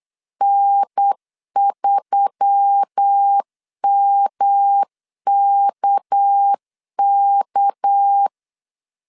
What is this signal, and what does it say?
Morse code, heard as a keyed tone, about 8 wpm, 800 Hz. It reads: N3MKK
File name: LEZIONI / 6 PKX0 / 6 PK 8wpm 3003